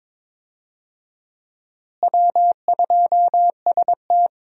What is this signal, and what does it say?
Morse code, heard as a keyed tone, about 22 wpm, 700 Hz. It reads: W2ST
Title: USA Call Signs - 4 Characters - Repeated 2x - 22wpm